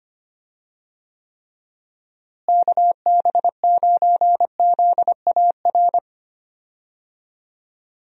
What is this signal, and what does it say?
Morse code, heard as a keyed tone, about 25 wpm, 700 Hz. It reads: KB9ZAR
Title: USA Call Signs - 6 Characters - Repeated 3x - 25wpm